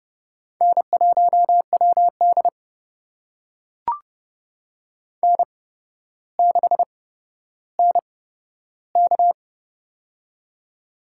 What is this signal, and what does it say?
Morse code, heard as a keyed tone, about 30 wpm, spaced as for 13 wpm, 700 Hz. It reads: N1WD N6NK
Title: USA Call Signs - 4 Characters - 8x Character Spacing - 30wpm